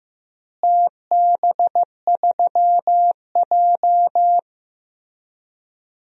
Morse code, heard as a keyed tone, about 15 wpm, 700 Hz. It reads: TB3J
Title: DXCC-100 Call Signs - 4 Characters - Repeated 3x - 15wpm